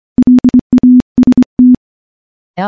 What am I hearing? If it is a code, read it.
Morse code, heard as a keyed tone, about 24 wpm, 250 Hz. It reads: LAST
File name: morse-training-25WPM